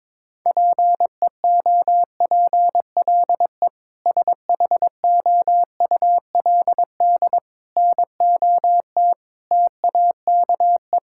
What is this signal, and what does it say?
Morse code, heard as a keyed tone, about 22 wpm, 700 Hz. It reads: PEOPLE SHOULD NOT TAKE
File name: practice / 22wpm / Sentences with 8-Word Phrases - 22wpm